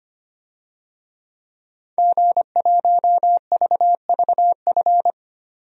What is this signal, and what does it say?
Morse code, heard as a keyed tone, about 25 wpm, 700 Hz. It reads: G1VVF
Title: DXCC-100 Call Signs - 5 Characters - 25wpm